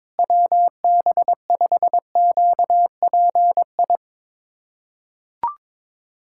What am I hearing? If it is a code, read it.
Morse code, heard as a keyed tone, about 22 wpm, 700 Hz. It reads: WB5QPI
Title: USA Callsign DE Callsign - 6 Characters - 22wpm